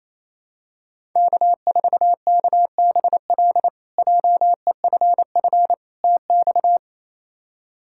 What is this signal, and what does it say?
Morse code, heard as a keyed tone, about 28 wpm, 700 Hz. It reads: K4KBL JEFF TX
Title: K1USN SST QSO Practice - 28wpm